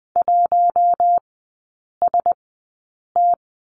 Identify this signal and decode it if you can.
Morse code, heard as a keyed tone, about 20 wpm, spaced as for 8 wpm, 700 Hz. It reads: 1ST
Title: 041 Lesson 8e - Number 1 - Three Characters 20wpm